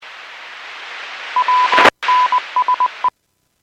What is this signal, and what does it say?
Morse code, heard as a keyed tone, about 20 wpm, 1000 Hz. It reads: WNSE